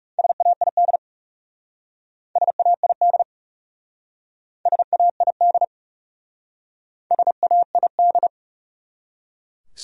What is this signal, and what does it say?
Morse code, heard as a keyed tone, about 36 wpm, 700 Hz. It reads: SAID SAID SAID SAID
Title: Speed-Racing Top 100 Words 30wpm